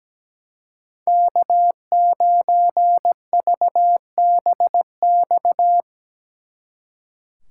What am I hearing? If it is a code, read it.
Morse code, heard as a keyed tone, about 19 wpm, 700 Hz. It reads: K9VBX
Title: Speed Racing - USA 1x3 Call Signs 17wpm